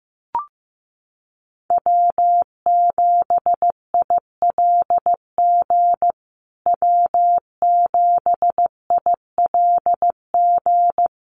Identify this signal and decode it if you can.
Morse code, heard as a keyed tone, about 15 wpm, 700 Hz. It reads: W7ILG W7ILG
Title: USA Call Signs Repeated 3x 15wpm